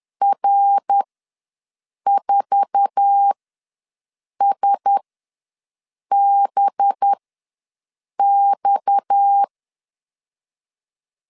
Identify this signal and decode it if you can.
Morse code, heard as a keyed tone, about 10 wpm, spaced as for 6 wpm, 800 Hz. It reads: R4SBX
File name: LEZIONI / 8 YJ45 / 8 YJ4 ws 10wpm 4020